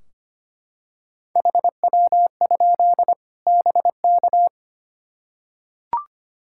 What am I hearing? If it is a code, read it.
Morse code, heard as a keyed tone, about 25 wpm, 700 Hz. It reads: HW? BK
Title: Practice QSO - Exchange 2 - Info - 25wpm